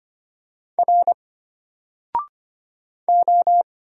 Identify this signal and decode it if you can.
Morse code, heard as a keyed tone, about 25 wpm, 700 Hz. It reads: R O
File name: Single-Letter-Number-25wpm